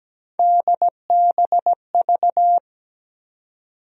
Morse code, heard as a keyed tone, about 17 wpm, 700 Hz. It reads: DBV
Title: USA Call Sign Postfixes - 3 Letters 17wpm